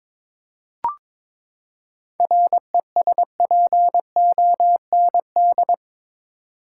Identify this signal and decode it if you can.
Morse code, heard as a keyed tone, about 22 wpm, 700 Hz. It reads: RESPOND
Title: Seven Letter Words - 22wpm